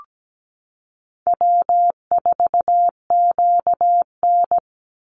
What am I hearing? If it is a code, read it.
Morse code, heard as a keyed tone, about 17 wpm, 700 Hz. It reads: W4QN
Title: Easy Call Signs - 17wpm